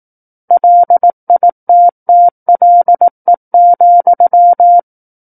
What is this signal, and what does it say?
Morse code, heard as a keyed tone, about 18 wpm, 700 Hz. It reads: LITTLE,
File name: 18 WPM